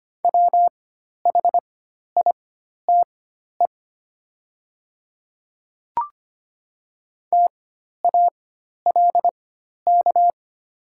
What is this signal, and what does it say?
Morse code, heard as a keyed tone, about 25 wpm, spaced as for 12 wpm, 700 Hz. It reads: WHITE TALK
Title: 4x Character Spacing - Top 500 Words - 25wpm